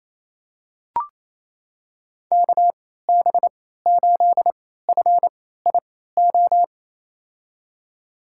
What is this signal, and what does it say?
Morse code, heard as a keyed tone, about 28 wpm, spaced as for 16 wpm, 700 Hz. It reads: KB8FIO
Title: USA Call Signs - 6 Characters - 3x Character Spacing - 28wpm